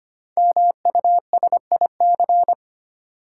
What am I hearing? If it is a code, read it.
Morse code, heard as a keyed tone, about 25 wpm, 700 Hz. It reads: MUSIC
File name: 5 Letter Words - First 3 Letters Given - 25wpm